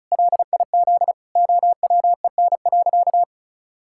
Morse code, heard as a keyed tone, about 34 wpm, 700 Hz. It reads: LIZ OWEN.
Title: Pangram Challenge v2 - 35wpm